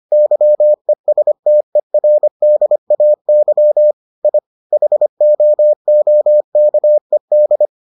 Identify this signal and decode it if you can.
Morse code, heard as a keyed tone, about 25 wpm, 600 Hz. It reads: YESTERDAY I HOOKED